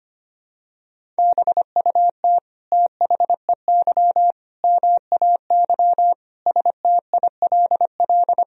Morse code, heard as a keyed tone, about 25 wpm, 700 Hz. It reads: BUT THEY MAY STILL
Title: Sentences from Top 300 Words and Abbreviations 25wpm